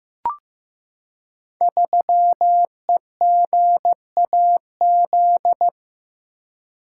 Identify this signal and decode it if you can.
Morse code, heard as a keyed tone, about 15 wpm, 700 Hz. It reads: 3EGAZ
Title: Code Groups - 5 Letter:Numbers - 15wpm